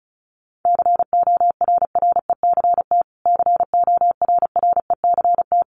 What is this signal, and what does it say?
Morse code, heard as a keyed tone, about 34 wpm, 700 Hz. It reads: CORRECT CORRECT
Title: Top 500 Words Repeated 35wpm